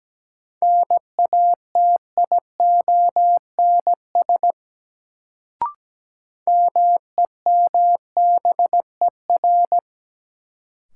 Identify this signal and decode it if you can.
Morse code, heard as a keyed tone, about 17 wpm, 700 Hz. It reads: NATIONS MEMBER
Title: Sets of 3 Words - New Word Drill - 17wpm